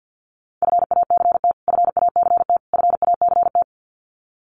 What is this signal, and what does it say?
Morse code, heard as a keyed tone, about 50 wpm, 700 Hz. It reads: FACT FACT FACT
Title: Top 500 Words Repeated 50wpm